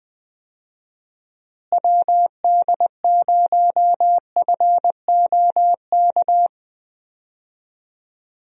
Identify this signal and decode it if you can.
Morse code, heard as a keyed tone, about 20 wpm, 700 Hz. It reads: WD0FOK
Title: USA Call Signs - 6 Characters - Repeated 3x - 20wpm